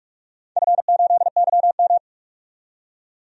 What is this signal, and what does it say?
Morse code, heard as a keyed tone, about 43 wpm, 700 Hz. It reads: F8YM
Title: DXCC-100 Call Signs - 4 Characters - 45wpm